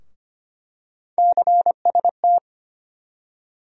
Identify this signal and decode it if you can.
Morse code, heard as a keyed tone, about 25 wpm, 700 Hz. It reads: CST
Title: USA Call Sign Postfixes - 3 Letters 25wpm